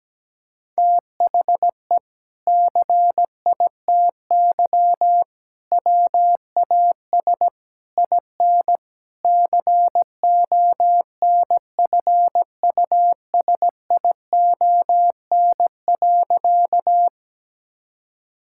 Morse code, heard as a keyed tone, about 17 wpm, 700 Hz. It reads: THE CITY WAS IN CONFUSION.
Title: Sentences from Top 1600 Words 17wpm